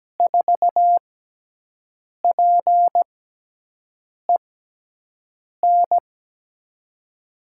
Morse code, heard as a keyed tone, about 17 wpm, spaced as for 6 wpm, 700 Hz. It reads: 4PEN
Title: USA Call Signs - 6 Characters - 6x Character Spacing - 17wpm